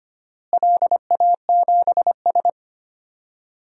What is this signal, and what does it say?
Morse code, heard as a keyed tone, about 25 wpm, 700 Hz. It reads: LA7S
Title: DXCC-100 Call Signs - 4 Characters - Repeated 3x - 25wpm